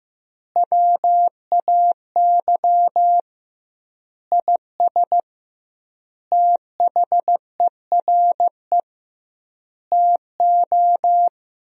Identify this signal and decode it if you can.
Morse code, heard as a keyed tone, about 15 wpm, 700 Hz. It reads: WAY IS THERE TO
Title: 2x Word-Spacing - Sentences from Top 100 Words - 15wpm